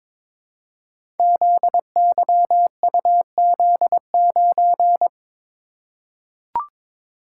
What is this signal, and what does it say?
Morse code, heard as a keyed tone, about 21 wpm, 700 Hz. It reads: ZYUZ9
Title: Code Groups - 5 Letter:Numbers - 22wpm